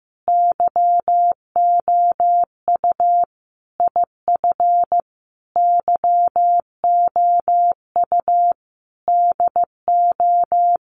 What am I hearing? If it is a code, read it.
Morse code, heard as a keyed tone, about 15 wpm, 700 Hz. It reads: YOU IF YOU DO
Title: Sentences from Top 1500 Words 15wpm